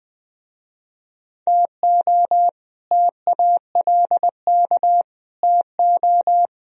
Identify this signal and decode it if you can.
Morse code, heard as a keyed tone, about 20 wpm, 700 Hz. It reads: TO TALK TO
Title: Speed-Racing - Sets of 4 Words 17wpm